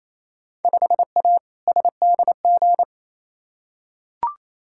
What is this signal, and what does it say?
Morse code, heard as a keyed tone, about 28 wpm, 700 Hz. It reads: 5A SDG
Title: Field Day Exchanges Encore - 28wpm